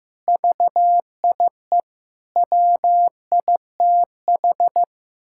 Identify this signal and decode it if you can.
Morse code, heard as a keyed tone, about 15 wpm, 700 Hz. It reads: VIE WITH
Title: Pangram Challenge v2 - 15wpm